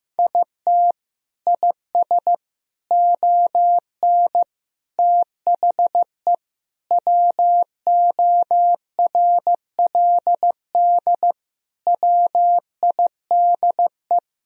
Morse code, heard as a keyed tone, about 15 wpm, 700 Hz. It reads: IT IS ON THE WORLD WIDE
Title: Sentences from Top 1000 Words - Review Part 4 - 15wpm